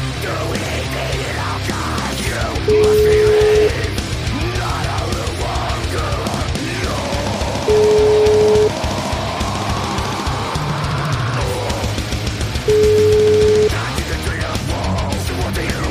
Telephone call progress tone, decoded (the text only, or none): ringback tone